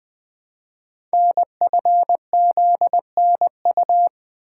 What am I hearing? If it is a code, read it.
Morse code, heard as a keyed tone, about 20 wpm, 700 Hz. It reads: NFZNU
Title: Code Groups - 5 Letters - 20wpm